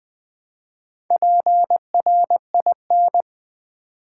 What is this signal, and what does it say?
Morse code, heard as a keyed tone, about 20 wpm, 700 Hz. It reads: PRIN
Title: Build A Word - 4 Letters At A Time - 8-Letter Words - 20wpm